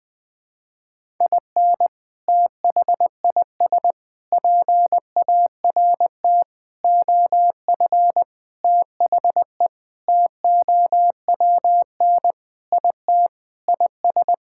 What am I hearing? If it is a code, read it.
Morse code, heard as a keyed tone, about 20 wpm, 700 Hz. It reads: IN THIS PART OF THE TOWN IT IS
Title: Sentences with 5-Word Phrases - 20wpm